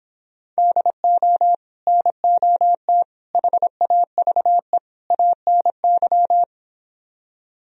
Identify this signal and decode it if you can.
Morse code, heard as a keyed tone, about 26 wpm, 700 Hz. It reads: DO NOT HAVE ANY
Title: Speed-Racing - Sets of 4 Words 22wpm